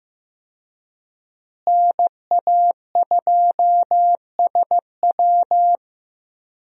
Morse code, heard as a keyed tone, about 15 wpm, 700 Hz. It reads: NA2SW
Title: USA Call Signs - 5 Characters - Repeated 3x - 15wpm